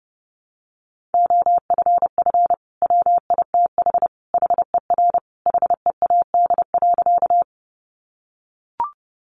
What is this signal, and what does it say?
Morse code, heard as a keyed tone, about 30 wpm, 700 Hz. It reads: OFF WITH HER HEAD.
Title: Sentences from Top 300 Words 30wpm